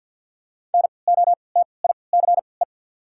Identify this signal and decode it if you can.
Morse code, heard as a keyed tone, about 50 wpm, spaced as for 28 wpm, 700 Hz. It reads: NOTICE
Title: 3x Character Spacing - Top 500 Words - 50wpm